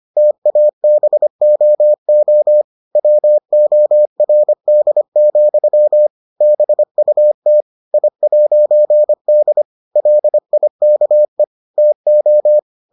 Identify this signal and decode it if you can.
Morse code, heard as a keyed tone, about 25 wpm, 600 Hz. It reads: TABOO WORD, BUT I'D LIKE TO